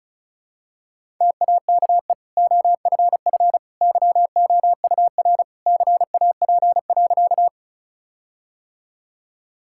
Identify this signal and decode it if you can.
Morse code, heard as a keyed tone, about 34 wpm, 700 Hz. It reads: TAKE OFF YOUR CAP.
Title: Sentences from Top 1600 Words 35wpm